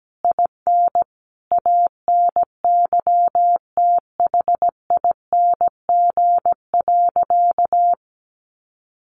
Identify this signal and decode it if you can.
Morse code, heard as a keyed tone, about 17 wpm, 700 Hz. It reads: IN ANYTHING.
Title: Sentences from Top 1500 Words 17wpm